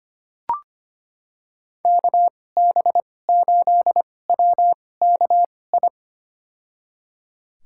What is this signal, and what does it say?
Morse code, heard as a keyed tone, about 25 wpm, spaced as for 18 wpm, 700 Hz. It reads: KB8WKI
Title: USA Call Signs - 6 Characters - 2x Character Spacing - 25wpm